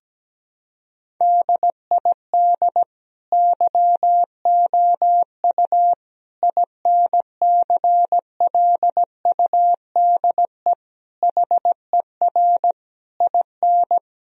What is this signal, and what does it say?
Morse code, heard as a keyed tone, about 17 wpm, 700 Hz. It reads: DID YOU INCLUDE HER IN
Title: Sentences from Top 1000 Words - Review Part 3 - 17wpm